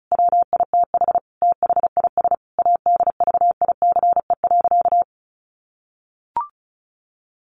Morse code, heard as a keyed tone, about 34 wpm, 700 Hz. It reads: WITH THIS ADVICE.
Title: Sentences from Top 1100 Words 35wpm